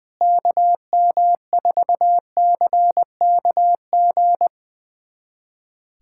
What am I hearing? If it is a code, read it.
Morse code, heard as a keyed tone, about 20 wpm, 700 Hz. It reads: KM4CKG
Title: USA Callsign DE Callsign 20wpm